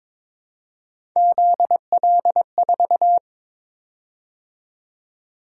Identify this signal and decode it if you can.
Morse code, heard as a keyed tone, about 22 wpm, 700 Hz. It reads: ZL4
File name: practice / 22wpm / Three Letter-Number Combinations - 22wpm